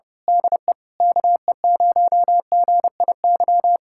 Morse code, heard as a keyed tone, about 24 wpm, 700 Hz. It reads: DE KE0GIY
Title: Speed-Racing Common QSO Phrases - 25wpm